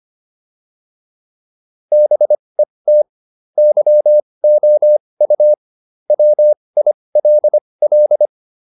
Morse code, heard as a keyed tone, about 25 wpm, spaced as for 20 wpm, 600 Hz. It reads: BET YOU WILL